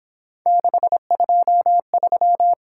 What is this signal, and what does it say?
Morse code, heard as a keyed tone, about 27 wpm, 700 Hz. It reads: 623
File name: Speed-Racing - Three Letter-Number Combinations - 22wpm